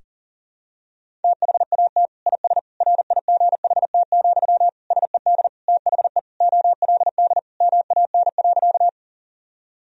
Morse code, heard as a keyed tone, about 40 wpm, 700 Hz. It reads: THAT IS RIGHT, SED THE OLD MAN.
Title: Sentences from Top 200 Words and Abbreviations 40wpm